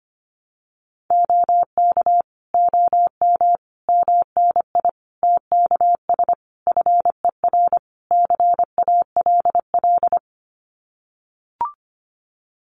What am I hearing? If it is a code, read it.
Morse code, heard as a keyed tone, about 25 wpm, 700 Hz. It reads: OK OM MNI TKS FER CALL
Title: Common QSO Phrases 25wpm